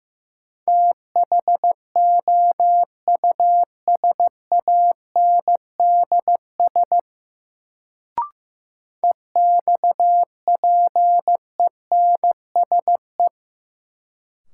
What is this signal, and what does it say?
Morse code, heard as a keyed tone, about 15 wpm, 700 Hz. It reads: THOUSANDS EXPENSE
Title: Sets of 4 Words - New Word Drill - 15wpm